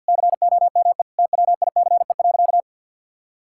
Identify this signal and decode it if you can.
Morse code, heard as a keyed tone, about 51 wpm, 700 Hz. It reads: COME TWICE.
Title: Sentences from Top 1600 Words 50wpm